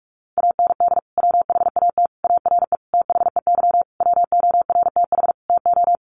Morse code, heard as a keyed tone, about 44 wpm, 700 Hz. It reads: AND WHAT ARE THEY WORTH TO
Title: Sentences from Top 800 Words 45wpm